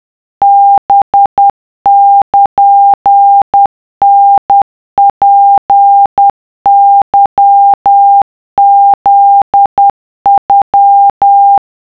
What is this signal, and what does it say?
Morse code (Japanese Wabun code), heard as a keyed tone, about 10 wpm, 800 Hz. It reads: ハルタツケフノ